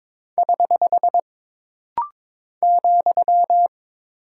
Morse code, heard as a keyed tone, about 22 wpm, 700 Hz. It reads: <HH> ,
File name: practice / 22wpm / Single Punctuation-Prosign - Rapid-Fire - 22wpm